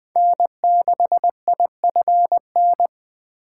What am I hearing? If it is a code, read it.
Morse code, heard as a keyed tone, about 20 wpm, 700 Hz. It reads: N6IFN
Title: USA Call Signs - 2x Character Spacing - 20wpm